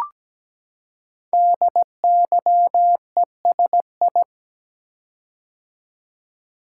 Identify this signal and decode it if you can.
Morse code, heard as a keyed tone, about 17 wpm, 700 Hz. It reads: DYESI
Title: Code Groups - 5 Letters - 17wpm